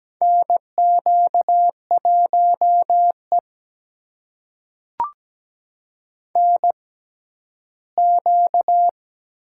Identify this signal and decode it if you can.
Morse code, heard as a keyed tone, about 17 wpm, spaced as for 11 wpm, 700 Hz. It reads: NQ1E NQ
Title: USA Call Signs - 4 Characters - 6x Character Spacing - 17wpm